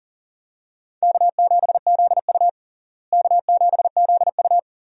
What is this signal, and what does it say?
Morse code, heard as a keyed tone, about 37 wpm, 700 Hz. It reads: K7ZU K7ZU
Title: USA Call Signs - 4 Characters - Repeated 2x - 40wpm